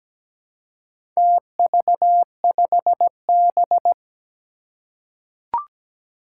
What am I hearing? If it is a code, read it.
Morse code, heard as a keyed tone, about 17 wpm, 700 Hz. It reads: TV5B
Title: Four Challenging Letter-Number Combinations 17wpm